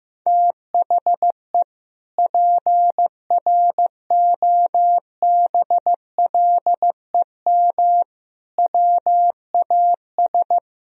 Morse code, heard as a keyed tone, about 15 wpm, 700 Hz. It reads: THE PROBLEM WAS
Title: Sentences from Top 1000 Words - Review Part 2 - 15wpm